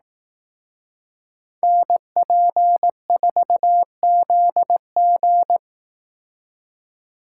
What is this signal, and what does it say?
Morse code, heard as a keyed tone, about 18 wpm, 700 Hz. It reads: NP4ZG
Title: Speed Racing - USA 2x2 Call Signs 15wpm